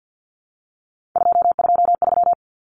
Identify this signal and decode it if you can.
Morse code, heard as a keyed tone, about 50 wpm, 700 Hz. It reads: 223
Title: Common Cut Numbers - 3-Digit 50wpm